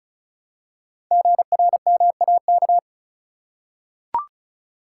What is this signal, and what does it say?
Morse code, heard as a keyed tone, about 36 wpm, 690 Hz. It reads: GRMAK
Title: Code Groups - 5 Letters - 35wpm